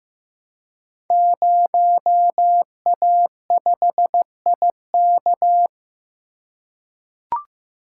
Morse code, heard as a keyed tone, about 15 wpm, 700 Hz. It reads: 0A5IK
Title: Code Groups - 5 Letter:Numbers - 15wpm